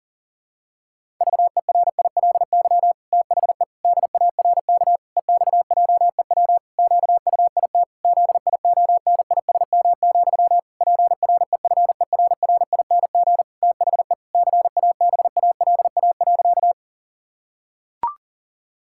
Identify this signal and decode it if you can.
Morse code, heard as a keyed tone, about 37 wpm, 700 Hz. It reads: VERILY THE DARK EXJEW QUIT ZIONISM, PREFERRING THE CABALA.
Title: Pangram Challenge v2 - 40wpm